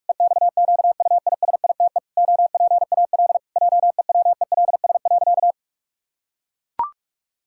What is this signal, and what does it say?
Morse code, heard as a keyed tone, about 41 wpm, 700 Hz. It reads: EXQUISITE OPAL JEWELS.
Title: Pangram Challenge v2 - 45wpm